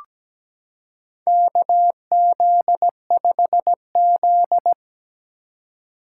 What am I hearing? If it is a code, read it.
Morse code, heard as a keyed tone, about 17 wpm, 700 Hz. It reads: KZ5Z
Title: K1USN SST QSO Practice - 17wpm